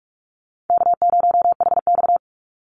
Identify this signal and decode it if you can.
Morse code, heard as a keyed tone, about 44 wpm, 700 Hz. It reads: K0HX
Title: Easy Call Signs Repeated 2x - 45wpm